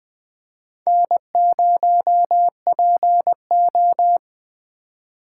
Morse code, heard as a keyed tone, about 21 wpm, 700 Hz. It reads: N0PO
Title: Speed Racing - USA 1x2 Call Signs 17wpm